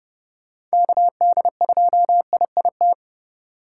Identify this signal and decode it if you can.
Morse code, heard as a keyed tone, about 40 wpm, 700 Hz. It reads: KD2IIT